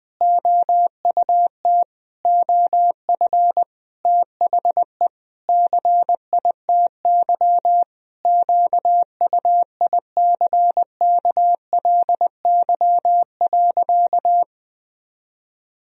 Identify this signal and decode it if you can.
Morse code, heard as a keyed tone, about 20 wpm, 700 Hz. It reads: OUT OF THE CITY QUICKLY.
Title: Sentences from Top 1000 Words - Review Part 1 - 20wpm